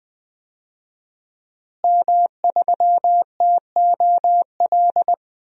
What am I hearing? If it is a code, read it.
Morse code, heard as a keyed tone, about 20 wpm, 700 Hz. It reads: M3TOL
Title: DXCC-100 Call Signs - 5 Characters - Repeated 3x - 20wpm